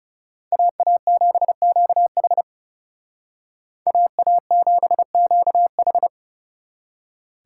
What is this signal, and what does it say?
Morse code, heard as a keyed tone, about 38 wpm, 700 Hz. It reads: AA7QH AA7QH